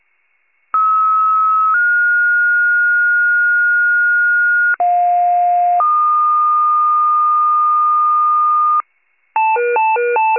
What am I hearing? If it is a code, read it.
Two-tone page, heard as a one-second tone, then a three-second tone, about 1290 then 1430 Hz, about 710 then 1220 Hz. It is heard from 0.7 to 8.8 s.
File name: wcalert